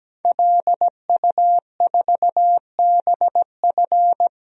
Morse code, heard as a keyed tone, about 17 wpm, 700 Hz. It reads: LU4BF
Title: DXCC-100 Call Signs - 5 Characters - 17wpm